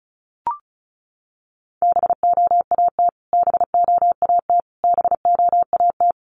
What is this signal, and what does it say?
Morse code, heard as a keyed tone, about 34 wpm, 700 Hz. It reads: BOAT BOAT BOAT
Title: Top 500 Words Repeated 35wpm